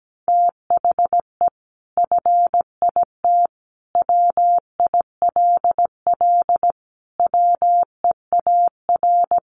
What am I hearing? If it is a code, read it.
Morse code, heard as a keyed tone, about 17 wpm, 700 Hz. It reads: THE FIT WILL WEAR
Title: Sentences from Top 1500 Words 17wpm